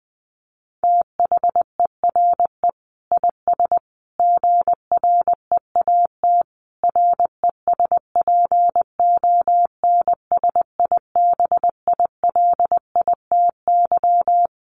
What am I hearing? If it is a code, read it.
Morse code, heard as a keyed tone, about 20 wpm, 700 Hz. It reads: THERE IS GREAT RESPONSIBILITY
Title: Sentences from Top 1200 Words 20wpm